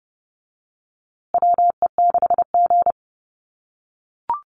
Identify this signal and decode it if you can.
Morse code, heard as a keyed tone, about 30 wpm, 700 Hz. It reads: WE6G
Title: USA Call Signs Repeated 3x 30wpm